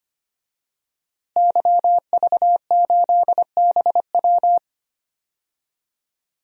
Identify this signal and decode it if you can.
Morse code, heard as a keyed tone, about 25 wpm, 700 Hz. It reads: YV8BW
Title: DXCC-100 Call Signs - 5 Characters - Repeated 2x - 25wpm